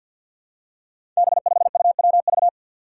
Morse code, heard as a keyed tone, about 48 wpm, 700 Hz. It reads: B5UWV
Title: Five Challenging Letter-Number Combinations 50wpm